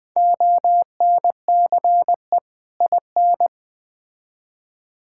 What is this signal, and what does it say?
Morse code, heard as a keyed tone, about 20 wpm, 700 Hz. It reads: ONCE IN
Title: Build A Phrase - 4-Word Phrases - 20wpm